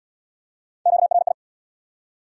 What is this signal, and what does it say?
Morse code, heard as a keyed tone, about 73 wpm, 700 Hz. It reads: QRI